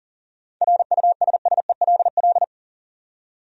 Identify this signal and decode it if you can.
Morse code, heard as a keyed tone, about 40 wpm, 700 Hz. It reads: RUSSELL
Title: USA Common Names - 40wpm